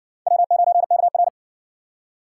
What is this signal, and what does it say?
Morse code, heard as a keyed tone, about 55 wpm, 700 Hz. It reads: W9KR